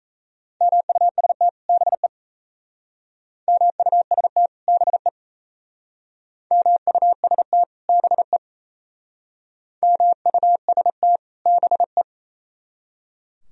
Morse code, heard as a keyed tone, about 38 wpm, 700 Hz. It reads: MUST BE MUST BE MUST BE MUST BE